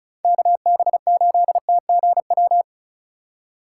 Morse code, heard as a keyed tone, about 44 wpm, 700 Hz. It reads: KB8TGW